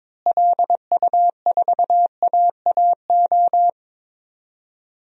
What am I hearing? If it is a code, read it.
Morse code, heard as a keyed tone, about 22 wpm, 700 Hz. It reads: LU4AAO
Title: DXCC-100 Call Signs - 6 Characters - Repeated 3x - 22wpm